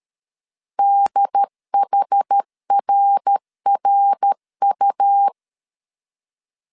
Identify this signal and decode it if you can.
Morse code, heard as a keyed tone, about 13 wpm, 800 Hz. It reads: DHRRU
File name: CW / DET / 1 3 1500C S C 12wpm 2750